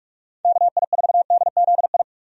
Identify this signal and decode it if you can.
Morse code, heard as a keyed tone, about 44 wpm, 700 Hz. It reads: KI4DZI
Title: USA Call Signs - 6 Characters - 45wpm